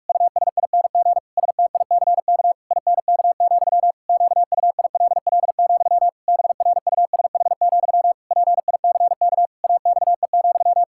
Morse code, heard as a keyed tone, about 46 wpm, 700 Hz. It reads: USING STICK INK, QUILL, BRUSH, PICK AXE,
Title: Pangram Challenge v2 - 45wpm